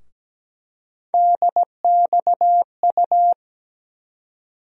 Morse code, heard as a keyed tone, about 17 wpm, 700 Hz. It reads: DXU
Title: USA Call Sign Postfixes - 3 Letters 17wpm